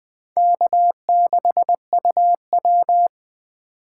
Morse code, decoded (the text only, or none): K6UW